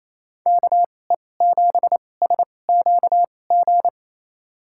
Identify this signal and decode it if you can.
Morse code, heard as a keyed tone, about 28 wpm, spaced as for 20 wpm, 700 Hz. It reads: KE7SQG
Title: USA Call Signs - 6 Characters - 2x Character Spacing - 28wpm